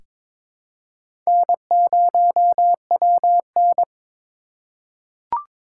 Morse code, heard as a keyed tone, about 22 wpm, 700 Hz. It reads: N0WN
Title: USA Call Signs - 4 Characters - 2x Character Spacing - 22wpm